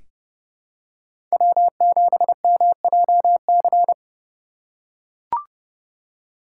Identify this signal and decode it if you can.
Morse code, heard as a keyed tone, about 32 wpm, 700 Hz. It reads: W7MJC
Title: USA Call Signs - 8x Character Spacing - 30wpm